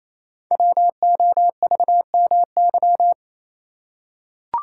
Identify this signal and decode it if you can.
Morse code, heard as a keyed tone, about 27 wpm, 700 Hz. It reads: WOVMY
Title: Code Groups - 5 Letters - 28wpm